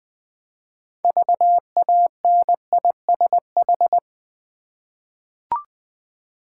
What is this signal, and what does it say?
Morse code, heard as a keyed tone, about 20 wpm, 700 Hz. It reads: VANISH
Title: 6 Letter Words - First 3 Letters Given - 20wpm